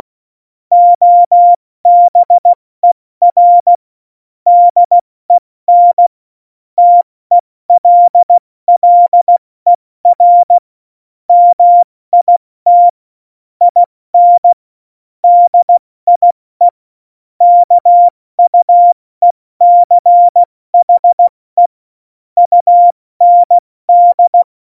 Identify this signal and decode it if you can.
Morse code, heard as a keyed tone, about 16 wpm, 700 Hz. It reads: OBER DEN TELLER MIT IN DIE KUECHE UND
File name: Morsekurs / Witze / Witz 26. November 2019, 70 F80